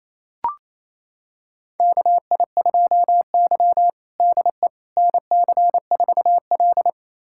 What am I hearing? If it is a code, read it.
Morse code, heard as a keyed tone, about 27 wpm, 700 Hz. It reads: KI2Y DE NC4L
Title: USA Callsign DE Callsign - 4 Characters - 28wpm